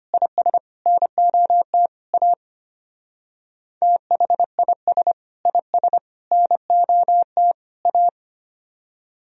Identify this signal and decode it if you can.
Morse code, heard as a keyed tone, about 31 wpm, 700 Hz. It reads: IS NOT A THIS IS NOT A
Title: Speed-Racing - Sets of 4 Words 25wpm